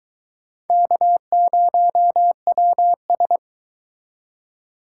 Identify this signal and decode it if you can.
Morse code, heard as a keyed tone, about 22 wpm, 700 Hz. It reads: K0WS